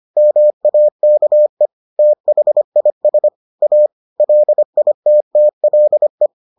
Morse code, heard as a keyed tone, about 25 wpm, 600 Hz. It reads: MAKE THIS A LITTLE